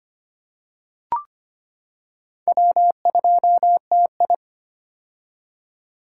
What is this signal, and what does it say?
Morse code, heard as a keyed tone, about 25 wpm, 700 Hz. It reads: W2TI
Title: K1USN SST QSO Practice - 25wpm